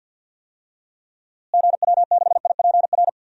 Morse code, heard as a keyed tone, about 48 wpm, 700 Hz. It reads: GW6IPR
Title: DXCC-100 Call Signs - 6 Characters - Repeated 3x - 50wpm